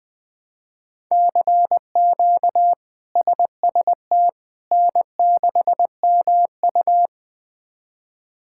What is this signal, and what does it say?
Morse code, heard as a keyed tone, about 20 wpm, 700 Hz. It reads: CQ SST N6MU
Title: K1USN SST QSO Practice - 20wpm